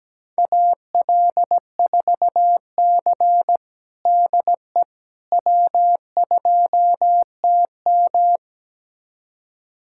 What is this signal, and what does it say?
Morse code, heard as a keyed tone, about 17 wpm, 700 Hz. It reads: AL4C DE W2TM
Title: USA Callsign DE Callsign - 4 Characters - 17wpm